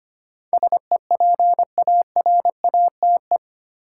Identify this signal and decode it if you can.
Morse code, heard as a keyed tone, about 25 wpm, 700 Hz. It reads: SEPARATE
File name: Eight Letter Words - 25wpm